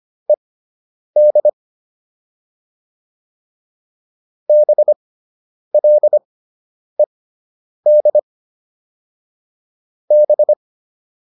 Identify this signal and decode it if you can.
Morse code, heard as a keyed tone, about 25 wpm, spaced as for 9 wpm, 600 Hz. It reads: ED BLED B